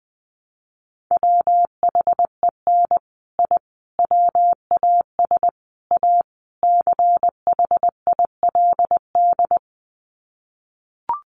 Morse code, heard as a keyed tone, about 20 wpm, 700 Hz. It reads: WHEN I WAS A CHILD
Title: Sets of 5 Words Encore 20wpm